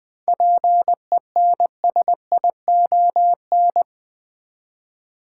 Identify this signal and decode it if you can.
Morse code, heard as a keyed tone, about 20 wpm, 700 Hz. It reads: PENSION
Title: Seven Letter Words - 20wpm